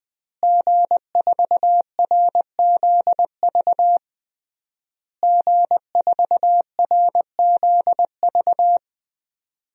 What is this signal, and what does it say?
Morse code, heard as a keyed tone, about 20 wpm, 700 Hz. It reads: G4RZV G4RZV
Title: DXCC-100 Call Signs - 5 Characters - Repeated 3x - 20wpm